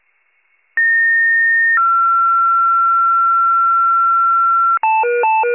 Two-tone page, heard as a one-second tone, then a three-second tone, about 1850 then 1360 Hz. It is heard from 0.8 to 4.8 s.